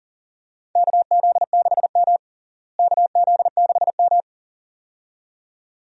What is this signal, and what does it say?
Morse code, heard as a keyed tone, about 40 wpm, 700 Hz. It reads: KZ6M KZ6M
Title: USA Call Signs - 4 Characters - Repeated 2x - 40wpm